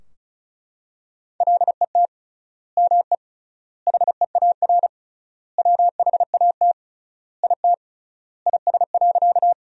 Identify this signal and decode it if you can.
Morse code, heard as a keyed tone, about 36 wpm, 700 Hz. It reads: LET ME HEAR WHAT IT IS.
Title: 3x Word-Spacing - Sentences from Top 300 Words - 35wpm